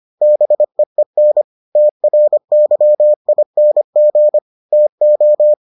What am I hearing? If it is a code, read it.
Morse code, heard as a keyed tone, about 25 wpm, 600 Hz. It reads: BEEN TRYING TO